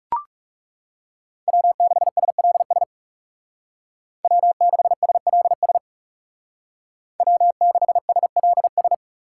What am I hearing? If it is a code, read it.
Morse code, heard as a keyed tone, about 36 wpm, 700 Hz. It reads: W6SLS W6SLS W6SLS
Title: Speed Racing - USA 1x3 Call Signs 30wpm